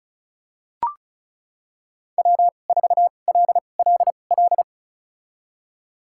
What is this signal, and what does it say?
Morse code, heard as a keyed tone, about 34 wpm, spaced as for 25 wpm, 700 Hz. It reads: W4LLL
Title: USA Call Signs - 5 Characters - 2x Character Spacing - 35wpm